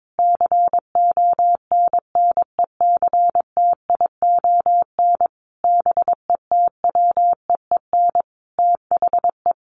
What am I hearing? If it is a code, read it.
Morse code, heard as a keyed tone, about 22 wpm, 700 Hz. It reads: CONNECTION BETWEEN THE
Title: Sentences from Top 1300 Words 22wpm